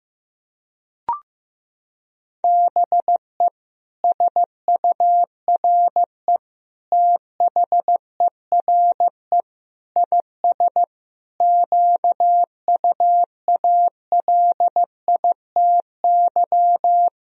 Morse code, heard as a keyed tone, about 16 wpm, 700 Hz. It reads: BE SURE THERE IS QUALITY
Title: Sentences from Top 1000 Words - Review Part 4 - 15wpm